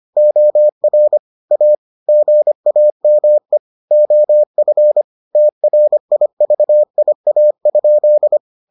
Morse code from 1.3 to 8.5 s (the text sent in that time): A GAME OF TRIVIA?